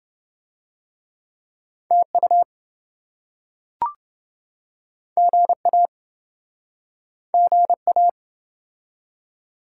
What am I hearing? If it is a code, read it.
Morse code, heard as a keyed tone, about 28 wpm, 700 Hz. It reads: TU GA GA